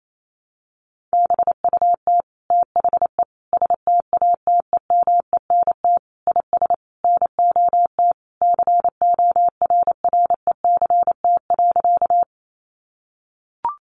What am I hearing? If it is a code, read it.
Morse code, heard as a keyed tone, about 28 wpm, 700 Hz. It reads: BUT THE STATEMENT IS NOT CORRECT.
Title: Sentences from Top 1200 Words 28wpm